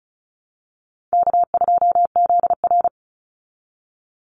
Morse code, heard as a keyed tone, about 34 wpm, 700 Hz. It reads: K2ZR
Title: Call Signs - 35wpm